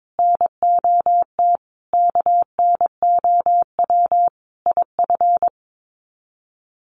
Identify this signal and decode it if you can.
Morse code, heard as a keyed tone, about 22 wpm, 700 Hz. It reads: NOT KNOW IF
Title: Sets of 5 Words Encore 22wpm